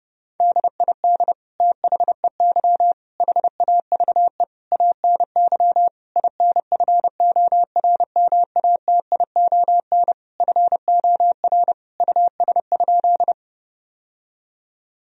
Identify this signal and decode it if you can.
Morse code, heard as a keyed tone, about 30 wpm, 700 Hz. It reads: DID THEY HAVE ANY INFORMATION FOR US?
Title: Sentences from Top 1000 Words - Review Part 3 - 30wpm